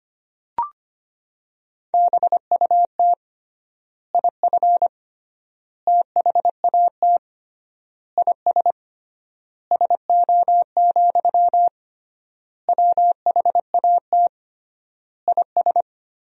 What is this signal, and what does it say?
Morse code, heard as a keyed tone, about 25 wpm, 700 Hz. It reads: BUT IF THAT IS SO, WHAT IS
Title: 3x Word-Spacing - Sentences from Top 200 Words - 25wpm